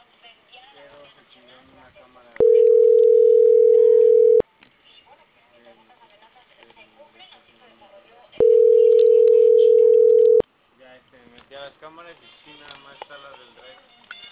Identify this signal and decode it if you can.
Ringback tone (North America), 440 + 480 Hz, with speech.
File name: CENTRAL CAEA / out agcaea02